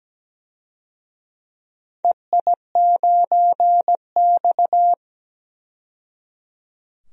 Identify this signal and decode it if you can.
Morse code, heard as a keyed tone, about 17 wpm, 700 Hz. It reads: EI9X
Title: DXCC-100 Call Signs - 4 Characters - 17wpm